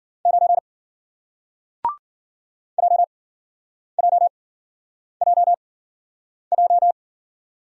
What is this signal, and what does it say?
Morse code, heard as a keyed tone, about 52 wpm, spaced as for 9 wpm, 700 Hz. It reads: 9 JJJJ